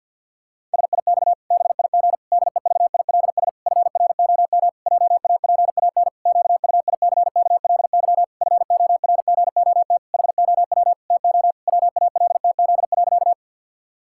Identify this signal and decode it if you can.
Morse code, heard as a keyed tone, about 51 wpm, 700 Hz. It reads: SIX BIG DEVILS FROM JAPAN QUICKLY FORGOT HOW TO WALTZ.
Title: Pangram Challenge v2 - 50wpm